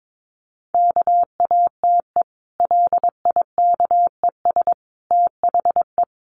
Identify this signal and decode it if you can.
Morse code, heard as a keyed tone, about 22 wpm, 700 Hz. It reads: KATE LIKES THE
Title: Easy Phrases 22wpm